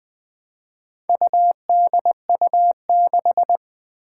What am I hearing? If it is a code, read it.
Morse code, heard as a keyed tone, about 20 wpm, 700 Hz. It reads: UDU6
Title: Four Challenging Letter-Number Combinations 20wpm